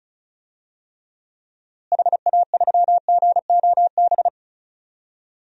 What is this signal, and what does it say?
Morse code, heard as a keyed tone, about 36 wpm, 700 Hz. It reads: HA3GOB
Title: DXCC-100 Call Signs - 6 Characters - Repeated 3x - 35wpm